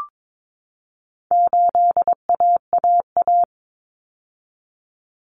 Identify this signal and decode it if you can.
Morse code, heard as a keyed tone, about 22 wpm, 700 Hz. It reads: 8AAA
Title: Common Cut Numbers - 4-Digit 22wpm